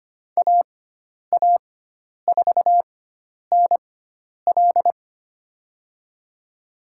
Morse code, heard as a keyed tone, about 25 wpm, spaced as for 10 wpm, 700 Hz. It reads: AA4NL
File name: Lesson 16g - Letter c - Call Signs 25wpm